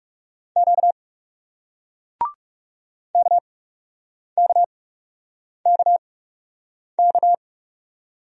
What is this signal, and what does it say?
Morse code, heard as a keyed tone, about 41 wpm, spaced as for 8 wpm, 700 Hz. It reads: Q KKKK